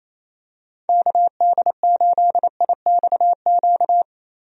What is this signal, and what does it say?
Morse code, heard as a keyed tone, about 28 wpm, 700 Hz. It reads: KD8IXQ
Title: USA Call Signs - 6 Characters - Repeated 2x - 28wpm